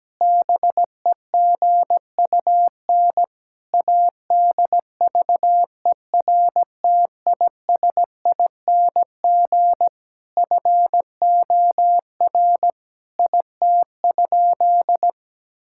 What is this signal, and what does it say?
Morse code, heard as a keyed tone, about 17 wpm, 700 Hz. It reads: BEGUN ADVERTISING FOR IT?
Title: Sentences from Top 1000 Words - Review Part 6 - 17wpm